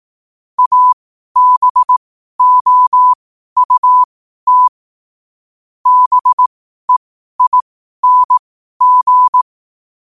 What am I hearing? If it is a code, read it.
Morse code, heard as a keyed tone, about 18 wpm, spaced as for 12 wpm, 1000 Hz. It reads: ABOUT BEING
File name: qotdfiles / 12wpm